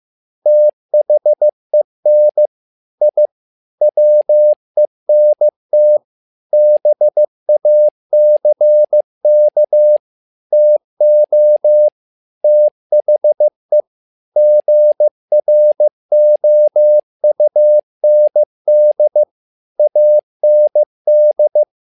Morse code, heard as a keyed tone, about 15 wpm, 600 Hz. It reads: THEN I WENT BACK TO THE GROUND AND
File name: learning center / tower trouble / tower trouble 15wpm